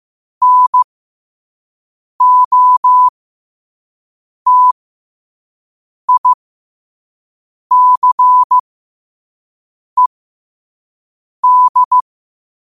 Morse code, heard as a keyed tone, about 15 wpm, spaced as for 5 wpm, 1000 Hz. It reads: NOTICED
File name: qotdfiles / 5wpm